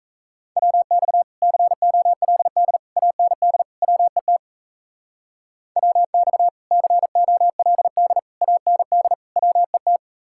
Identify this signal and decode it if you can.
Morse code, heard as a keyed tone, about 34 wpm, 700 Hz. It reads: WX COLD AND WET WX COLD AND WET
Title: Speed-Racing Common QSO Phrases - 28wpm